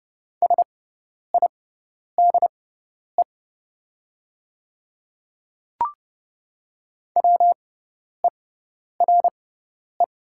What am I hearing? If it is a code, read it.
Morse code, heard as a keyed tone, about 30 wpm, spaced as for 10 wpm, 690 Hz. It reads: SIDE WERE
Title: 6x Character Spacing - Top 100 Words - 30wpm